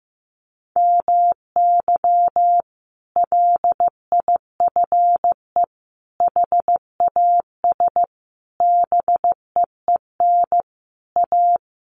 Morse code, heard as a keyed tone, about 15 wpm, 700 Hz. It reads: MY LIFE HAS BEEN A
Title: Sentences from Top 1100 Words 15wpm